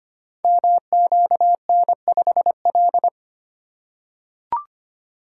Morse code, heard as a keyed tone, about 25 wpm, 700 Hz. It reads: MQN5L
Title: Code Groups - 5 Letter:Numbers - 25wpm